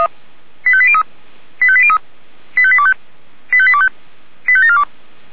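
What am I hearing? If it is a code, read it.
Five-tone selective call (ZVEI), heard as quick runs of five tones, tones of about 70 ms. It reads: 85803 85803 85825 85825 85742
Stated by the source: ZVEI five-tone selective call received by SDR.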